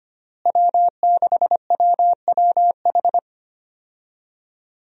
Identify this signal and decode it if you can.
Morse code, heard as a keyed tone, about 25 wpm, 700 Hz. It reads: W6WWH
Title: USA Call Signs - 5 Characters - Repeated 3x - 25wpm